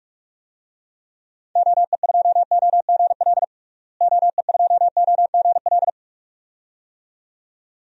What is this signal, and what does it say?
Morse code, heard as a keyed tone, about 45 wpm, 700 Hz. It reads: OE2OGL OE2OGL
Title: DXCC-100 Call Signs - 6 Characters - Repeated 2x - 45wpm